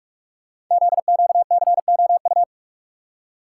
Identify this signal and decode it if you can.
Morse code, heard as a keyed tone, about 44 wpm, 690 Hz. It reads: ZQCOU
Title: Code Groups - 5 Letters - 45wpm